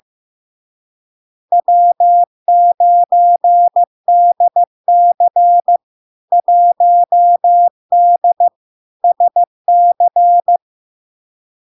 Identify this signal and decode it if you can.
Morse code, heard as a keyed tone, about 15 wpm, 700 Hz. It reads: W9DC 1D SC
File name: LARK-FD-2022-Log-in-CW-15wpm-700-Hz